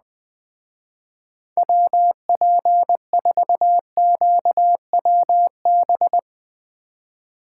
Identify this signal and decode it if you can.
Morse code, heard as a keyed tone, about 20 wpm, 700 Hz. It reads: WP4QWB